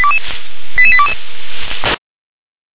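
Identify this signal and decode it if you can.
Five-tone selective call (ZVEI), heard as quick runs of five tones, tones of about 70 ms. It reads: 88822
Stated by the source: ZVEI five-tone selective call received by SDR.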